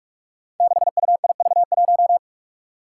Morse code, heard as a keyed tone, about 44 wpm, 700 Hz. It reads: 6UIV1
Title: Code Groups - 5 Letter:Numbers - 45wpm